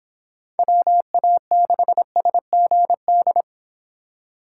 Morse code, heard as a keyed tone, about 30 wpm, 700 Hz. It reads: WA6SGD